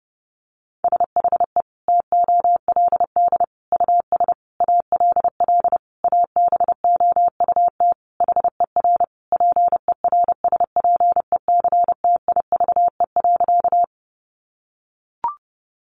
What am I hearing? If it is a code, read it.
Morse code, heard as a keyed tone, about 30 wpm, 700 Hz. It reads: SHE TOLD US ALL ABOUT HER PERSPECTIVE.
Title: Sentences from Top 1200 Words 30wpm